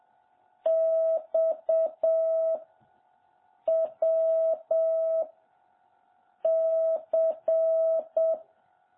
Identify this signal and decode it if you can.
Morse code, heard as a keyed tone, about 7 wpm, 660 Hz. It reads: X W C
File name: jh8cbh / 2021cw